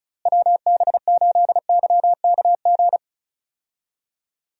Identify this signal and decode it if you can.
Morse code, heard as a keyed tone, about 34 wpm, 700 Hz. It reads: WB8YKG
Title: USA Call Signs - 6 Characters - Repeated 2x - 35wpm